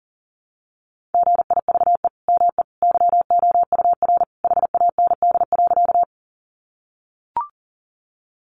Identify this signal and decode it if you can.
Morse code, heard as a keyed tone, about 40 wpm, 700 Hz. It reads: GIVE ME YOUR HAND.
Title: Sentences from Top 200 Words 40wpm